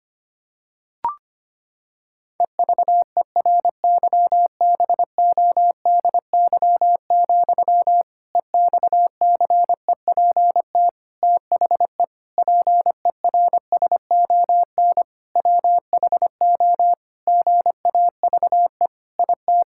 Morse code, heard as a keyed tone, about 25 wpm, 700 Hz. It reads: EVERYBODY, EXCEPT THE PERSON WHO GAVE IT
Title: Sentences from Top 1600 Words 25wpm